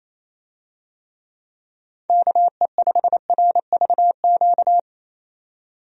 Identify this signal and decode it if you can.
Morse code, heard as a keyed tone, about 28 wpm, 700 Hz. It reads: KE5RVQ
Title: USA Call Signs - 6 Characters - 28wpm